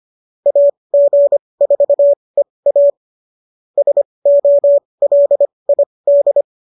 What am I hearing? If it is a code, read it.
Morse code, heard as a keyed tone, about 25 wpm, spaced as for 20 wpm, 570 Hz. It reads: AG4EA SOLID